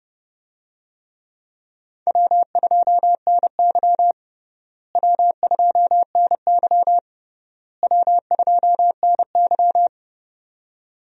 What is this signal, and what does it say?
Morse code, heard as a keyed tone, about 30 wpm, 700 Hz. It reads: W2NY W2NY W2NY
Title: USA Call Signs - 4 Characters - Repeated 3x - 30wpm